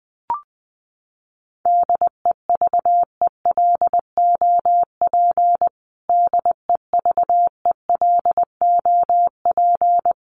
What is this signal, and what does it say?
Morse code, heard as a keyed tone, about 20 wpm, 700 Hz. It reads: DEVELOP DEVELOP
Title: Top 500 Words Repeated 20wpm